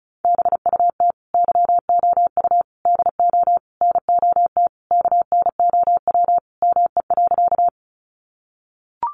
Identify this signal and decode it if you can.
Morse code, heard as a keyed tone, about 34 wpm, 700 Hz. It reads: BUT YOU DO NOT KNOW ME.
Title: Sentences from Top 600 Words 35wpm